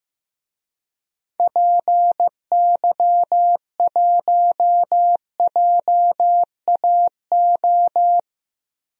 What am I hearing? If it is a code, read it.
Morse code, heard as a keyed tone, about 15 wpm, 700 Hz. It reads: PY1JAO
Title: DXCC-100 Call Signs - 6 Characters - Repeated 2x - 15wpm